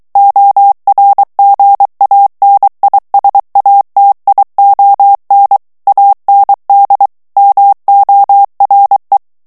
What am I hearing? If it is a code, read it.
Morse code, heard as a keyed tone, about 23 wpm, 800 Hz. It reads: ORGANISATION AND MORE
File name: morse mp3 / ANGLAIS texte clair 1 vitesse 22 mots minute